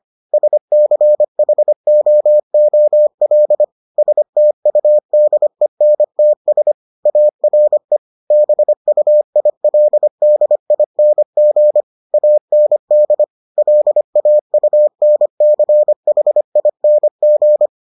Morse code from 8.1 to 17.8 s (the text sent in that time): BUILDING AND LAUNCHING